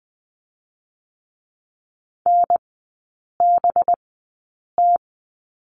Morse code, heard as a keyed tone, about 20 wpm, spaced as for 8 wpm, 700 Hz. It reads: NBT
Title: 184 Lesson 30e - Letter b - Three Characters 20wpm